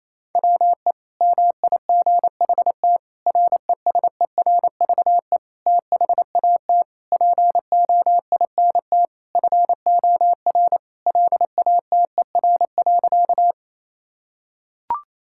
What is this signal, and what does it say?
Morse code, heard as a keyed tone, about 28 wpm, 700 Hz. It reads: WE MIGHT RESERVE THAT POINT FOR LATER.
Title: Sentences from Top 1000 Words - Review Part 6 - 28wpm